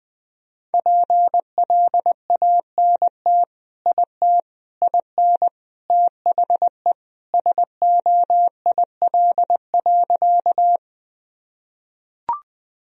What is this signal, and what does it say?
Morse code, heard as a keyed tone, about 20 wpm, 700 Hz. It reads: PLANT IT IN THE SOIL.
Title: Sentences from Top 1000 Words - Review Part 4 - 20wpm